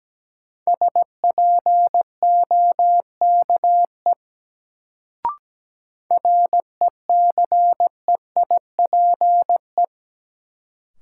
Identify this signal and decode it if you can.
Morse code, heard as a keyed tone, about 17 wpm, 700 Hz. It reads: SPOKE RECEIPE
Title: Sentences from Top 1200 Words - New Word Drill - 17wpm